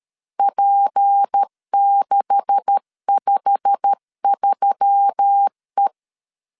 Morse code, heard as a keyed tone, about 13 wpm, 800 Hz. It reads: P653E
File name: LEZIONI / 9 286 12wpm 2305